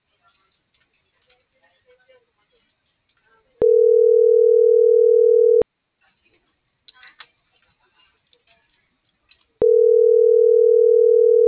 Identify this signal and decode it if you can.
Ringback tone (North America), 440 + 480 Hz.